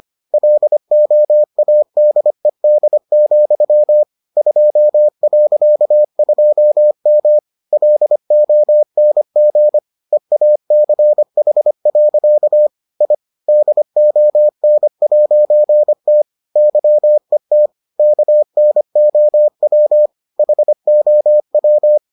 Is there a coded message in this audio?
Yes: Morse code from 0.3 to 22.0 s, heard as a keyed tone, about 25 wpm, 600 Hz. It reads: LOADED, 2.2M LONG EACH. I DON'T YET KNOW HOW